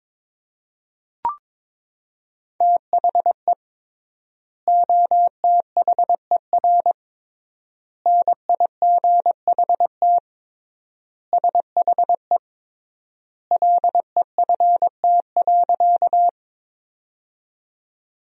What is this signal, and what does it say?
Morse code, heard as a keyed tone, about 22 wpm, 700 Hz. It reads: THE OTHER NIGHT SHE LEFT.
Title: 3x Word-Spacing - Sentences from Top 300 Words - 22wpm